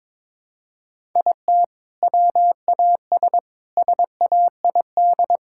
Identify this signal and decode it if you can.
Morse code, heard as a keyed tone, about 22 wpm, 700 Hz. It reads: IT WAS SAID
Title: Sentences from Top 1000 Words - Review Part 6 - 22wpm